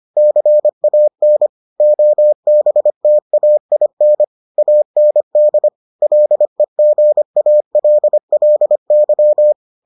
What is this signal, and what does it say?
Morse code, heard as a keyed tone, about 25 wpm, 600 Hz. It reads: CAN OBTAIN AND LEGALLY